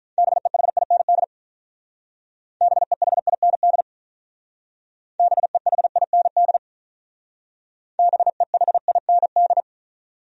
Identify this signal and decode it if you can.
Morse code, heard as a keyed tone, about 45 wpm, 700 Hz. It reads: BEHIND BEHIND BEHIND BEHIND